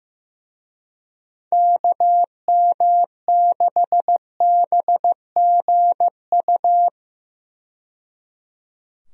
Morse code, heard as a keyed tone, about 15 wpm, 700 Hz. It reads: KM6BGU
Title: USA Call Signs - 6 Characters - 15wpm